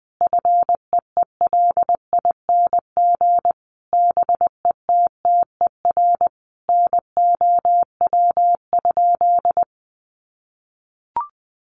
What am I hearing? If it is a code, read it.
Morse code, heard as a keyed tone, about 20 wpm, 700 Hz. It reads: FEELING BETTER NOW?
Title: Sentences from Top 1200 Words 20wpm